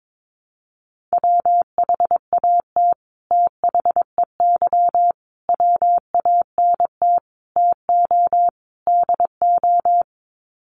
Morse code, heard as a keyed tone, about 22 wpm, 700 Hz. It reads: WHAT THEY WANT TO DO
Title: Sets of 5 Words Encore2 22wpm